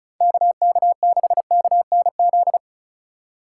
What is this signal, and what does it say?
Morse code, heard as a keyed tone, about 34 wpm, 690 Hz. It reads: KK6KNZ
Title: USA Call Signs - 6 Characters - 4x Character Spacing - 35wpm